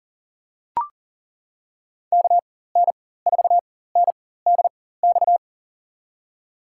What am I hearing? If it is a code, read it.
Morse code, heard as a keyed tone, about 40 wpm, spaced as for 19 wpm, 700 Hz. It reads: KN4NDX
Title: USA Call Signs - 6 Characters - 4x Character Spacing - 40wpm